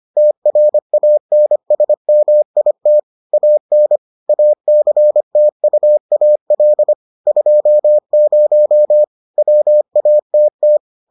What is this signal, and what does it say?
Morse code, heard as a keyed tone, about 25 wpm, 600 Hz. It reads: TRANSMIT AN ACTUAL 20 WATT